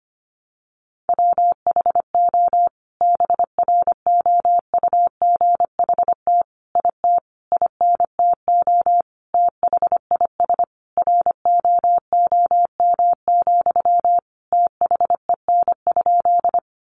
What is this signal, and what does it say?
Morse code, heard as a keyed tone, about 25 wpm, 700 Hz. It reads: WHO BROUGHT IT INTO THIS ROOM, THEN?
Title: Sentences from Top 600 Words 25wpm